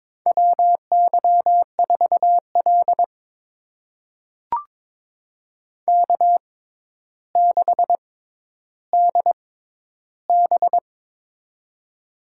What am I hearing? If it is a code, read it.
Morse code, heard as a keyed tone, about 22 wpm, spaced as for 11 wpm, 700 Hz. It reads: WY4L K6DB
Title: USA Call Signs - 4 Characters - 6x Character Spacing - 22wpm